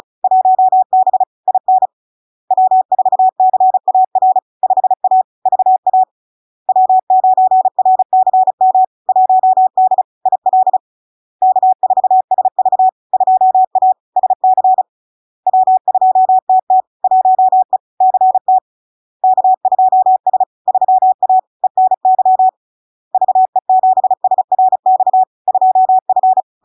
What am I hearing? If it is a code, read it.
Morse code, heard as a keyed tone, about 36 wpm, 750 Hz. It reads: 1B IN W4CAR 5A VA W9RCM 1D IL K4SV 2A SC W2TT 1E CT K2S 3A ENY VE7SRX 2F
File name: LARK-FD-2022-Log-in-CW-35wpm-750-Hz